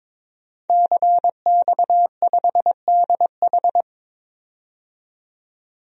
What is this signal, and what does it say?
Morse code, heard as a keyed tone, about 22 wpm, 700 Hz. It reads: CX5DH
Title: DXCC-100 Call Signs - 5 Characters - 22wpm